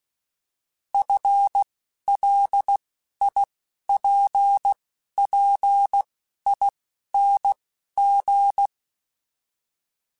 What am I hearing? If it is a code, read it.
Morse code, heard as a keyed tone, about 16 wpm, spaced as for 11 wpm, 800 Hz. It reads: FLIPPING
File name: FL / FL mp3s / Fast FL @10